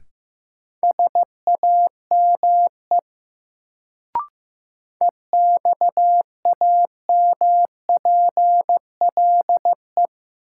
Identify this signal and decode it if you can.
Morse code, heard as a keyed tone, about 15 wpm, 700 Hz. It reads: SAME EXAMPLE
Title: Sets of 2 Words - New Word Drill - 15wpm